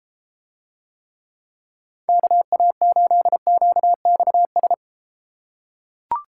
Morse code, heard as a keyed tone, about 33 wpm, 700 Hz. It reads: KA8QXS